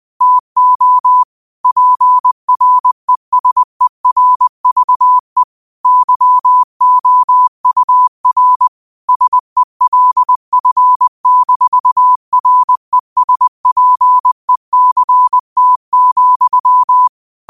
Morse code, heard as a keyed tone, about 20 wpm, 1000 Hz. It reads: TO PRESERVE YOUR SELF-RESPECT,
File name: 20wpm